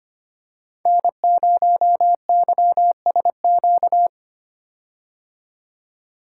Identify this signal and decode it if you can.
Morse code, heard as a keyed tone, about 25 wpm, 700 Hz. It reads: N0YSQ
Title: K1USN SST QSO Practice - 25wpm